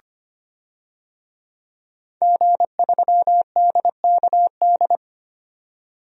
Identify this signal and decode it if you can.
Morse code, heard as a keyed tone, about 25 wpm, 700 Hz. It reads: G3DKD
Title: DXCC-100 Call Signs - 5 Characters - 25wpm